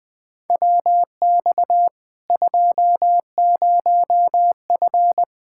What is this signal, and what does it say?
Morse code, heard as a keyed tone, about 20 wpm, 700 Hz. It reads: WX 20F
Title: K1USN SST QSO Practice 20wpm